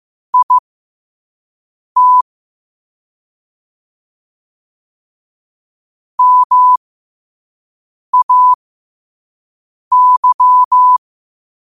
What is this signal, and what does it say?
Morse code, heard as a keyed tone, about 15 wpm, spaced as for 5 wpm, 1000 Hz. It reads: IT MAY
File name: qotdfiles / 5wpm